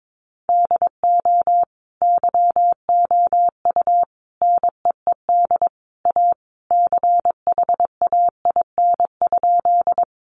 Morse code, heard as a keyed tone, about 22 wpm, 700 Hz. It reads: DO YOU NEED A CHAIN?
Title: Sentences from Top 1500 Words 22wpm